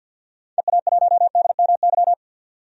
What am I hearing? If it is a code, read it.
Morse code, heard as a keyed tone, about 52 wpm, 700 Hz. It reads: EA1DMY
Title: DXCC-100 Call Signs - 6 Characters - 50wpm